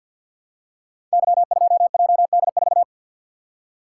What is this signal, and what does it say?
Morse code, heard as a keyed tone, about 51 wpm, 700 Hz. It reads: Y2JD4
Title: Code Groups - 5 Letter:Numbers - 50wpm